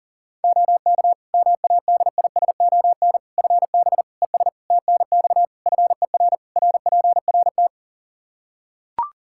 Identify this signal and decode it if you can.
Morse code, heard as a keyed tone, about 37 wpm, 700 Hz. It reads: OK MADISON FB ES TNX FER RPRT
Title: Practice QSO - Exchange 2 - Info - 40wpm